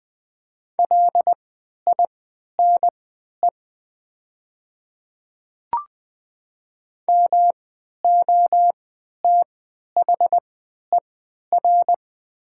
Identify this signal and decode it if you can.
Morse code, heard as a keyed tone, about 20 wpm, spaced as for 11 wpm, 700 Hz. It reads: LINE MOTHER
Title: 3x Character Spacing - Top 200 Words - 20wpm